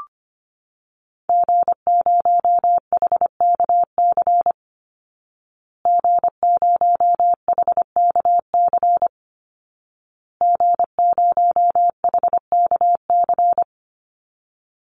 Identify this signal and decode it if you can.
Morse code, heard as a keyed tone, about 25 wpm, 700 Hz. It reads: G0HKC G0HKC G0HKC
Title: Call Signs Repeated 2x - 25wpm